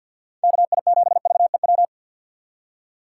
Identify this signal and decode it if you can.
Morse code, heard as a keyed tone, about 46 wpm, 700 Hz. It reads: KI7VEW
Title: USA Call Signs - 3x Character Spacing - 50wpm